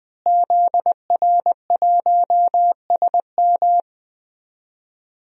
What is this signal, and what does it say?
Morse code, heard as a keyed tone, about 20 wpm, 700 Hz. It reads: ZR1SM
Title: DXCC-100 Call Signs - 5 Characters - Repeated 3x - 20wpm